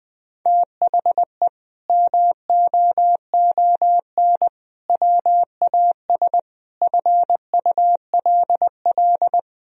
Morse code, heard as a keyed tone, about 20 wpm, 700 Hz. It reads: THE MOON WAS FULL
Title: Sentences from Top 1000 Words - Review Part 3 - 20wpm